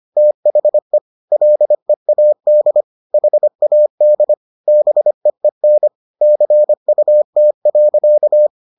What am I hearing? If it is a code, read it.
Morse code, heard as a keyed tone, about 25 wpm, 600 Hz. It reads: THE LEAD HAD BEEN CUT.